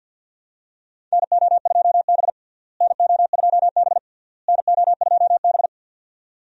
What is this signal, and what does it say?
Morse code, heard as a keyed tone, about 52 wpm, 700 Hz. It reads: NO2B NO2B NO2B
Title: USA Call Signs - 4 Characters - Repeated 3x - 50wpm